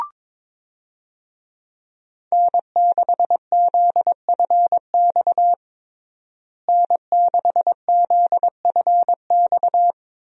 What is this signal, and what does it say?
Morse code, heard as a keyed tone, about 21 wpm, 700 Hz. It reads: N6ZFX N6ZFX
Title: USA Call Signs - 5 Characters - Repeated 3x - 22wpm